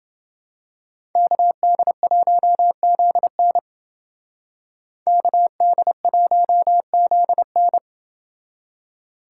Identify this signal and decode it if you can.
Morse code, heard as a keyed tone, about 29 wpm, 700 Hz. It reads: KD1ZN KD1ZN